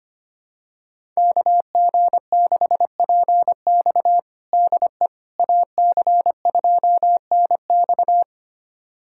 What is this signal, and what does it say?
Morse code, heard as a keyed tone, about 25 wpm, 700 Hz. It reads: KG6PX DE AC2NX
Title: USA Callsign DE Callsign - 5 Characters - 25wpm